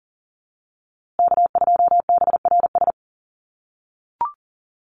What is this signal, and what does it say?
Morse code, heard as a keyed tone, about 40 wpm, 700 Hz. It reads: K2BRS
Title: USA Call Signs Repeated 2x 40wpm